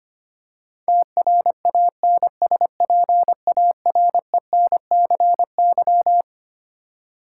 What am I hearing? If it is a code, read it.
Morse code, heard as a keyed tone, about 25 wpm, 700 Hz. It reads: TRANSPARENCY
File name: Twelve Letter Words - 25wpm